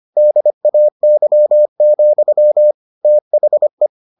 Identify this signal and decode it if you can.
Morse code, heard as a keyed tone, about 25 wpm, 600 Hz. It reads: DAY, THE